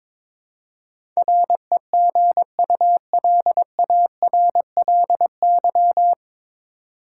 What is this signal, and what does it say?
Morse code, heard as a keyed tone, about 22 wpm, 700 Hz. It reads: REGULARLY
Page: Morse Code Ninja practice 22wpm